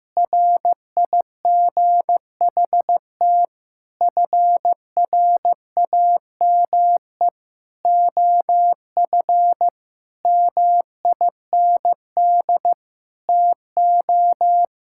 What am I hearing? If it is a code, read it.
Morse code, heard as a keyed tone, about 16 wpm, 700 Hz. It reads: RIGHT FRAME OF MIND TO
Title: Sentences from Top 1000 Words - Review Part 6 - 15wpm